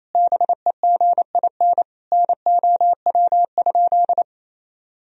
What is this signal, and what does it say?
Morse code, heard as a keyed tone, about 27 wpm, 700 Hz. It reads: BEGIN NOW?
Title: Sentences from Top 1000 Words - Review Part 1 - 28wpm